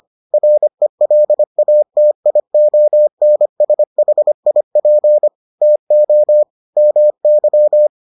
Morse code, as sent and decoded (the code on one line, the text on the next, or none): .-. .-.. .- - .. --- -. ... .... .. .--. / - --- / -- -.--
RELATIONSHIP TO MY